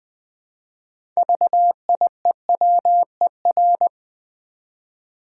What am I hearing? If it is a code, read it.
Morse code, heard as a keyed tone, about 20 wpm, 700 Hz. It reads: VIEWER
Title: Build A Word - 3 Letters At A Time - 6-Letter Words - 20wpm